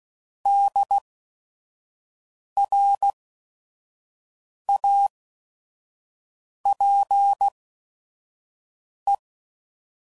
Morse code, heard as a keyed tone, about 16 wpm, spaced as for 5 wpm, 800 Hz. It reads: DRAPE
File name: Prefixes / By First Letter / Ds